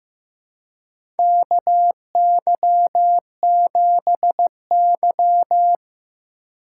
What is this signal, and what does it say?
Morse code, heard as a keyed tone, about 15 wpm, 700 Hz. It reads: KY7Y
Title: USA Call Signs - 4 Characters - 8x Character Spacing - 15wpm